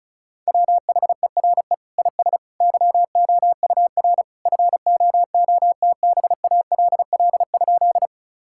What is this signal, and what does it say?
Morse code, heard as a keyed tone, about 34 wpm, 700 Hz. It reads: WHERE IS YOUR FOOTBALL?
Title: Sentences from Top 1600 Words 35wpm